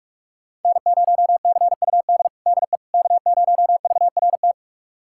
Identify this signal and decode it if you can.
Morse code, heard as a keyed tone, about 45 wpm, 700 Hz. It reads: N0CUD DE K0VRT
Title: USA Callsign DE Callsign - 5 Characters - 45wpm